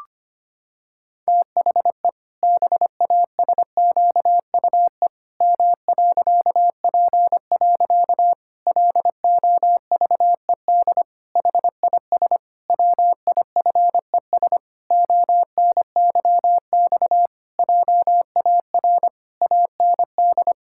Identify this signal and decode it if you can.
Morse code, heard as a keyed tone, about 25 wpm, 700 Hz. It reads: THE BASQUE M.P. LOVED HIS WIFES ONYX JAR AND
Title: Pangram Challenge v2 - 25wpm